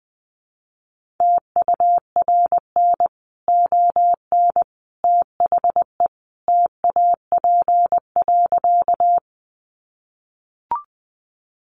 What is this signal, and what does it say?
Morse code, heard as a keyed tone, about 20 wpm, 700 Hz. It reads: TURN ON THE TAP.
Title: Sentences from Top 1400 Words 20wpm